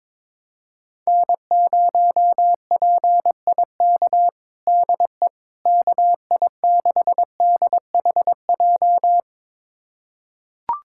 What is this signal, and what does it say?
Morse code, heard as a keyed tone, about 22 wpm, 700 Hz. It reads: N0PIK DE KI6DHJ
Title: USA Callsign DE Callsign 22wpm